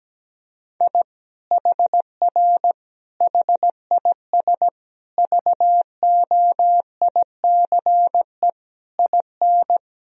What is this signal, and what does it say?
Morse code, heard as a keyed tone, about 17 wpm, 700 Hz. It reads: I HR HIS VOICE IN
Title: Sentences from Top 500 Words and Abbreviations 17wpm